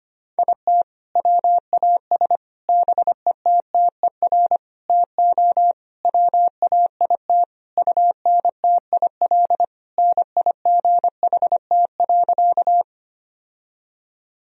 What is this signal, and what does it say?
Morse code, heard as a keyed tone, about 25 wpm, 700 Hz. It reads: IT WAS BETTER TO WAIT UNTIL NIGHT.
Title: Sentences from Top 1000 Words - Review Part 1 - 25wpm